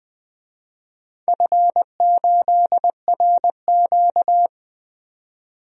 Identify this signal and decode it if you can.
Morse code, heard as a keyed tone, about 20 wpm, 700 Hz. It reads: F8RQ
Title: DXCC-100 Call Signs - 4 Characters - 20wpm